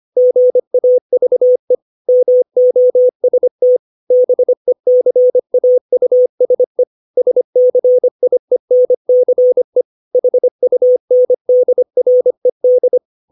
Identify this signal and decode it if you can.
Morse code, heard as a keyed tone, about 25 wpm, 500 Hz. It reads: GAVE MOST BECAUSE SCIENCE HUNDRED
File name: 632 words / 632 Words 25WPM